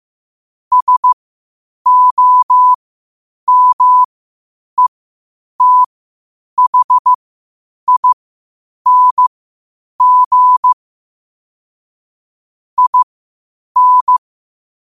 Morse code, heard as a keyed tone, about 15 wpm, spaced as for 8 wpm, 1000 Hz. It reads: SOMETHING IN
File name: qotdfiles / 8wpm